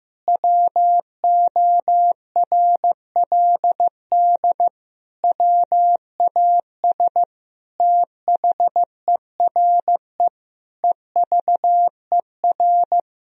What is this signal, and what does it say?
Morse code, heard as a keyed tone, about 15 wpm, 700 Hz. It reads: WORLD WAS THERE EVER
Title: Sentences from Top 1000 Words - Review Part 4 - 15wpm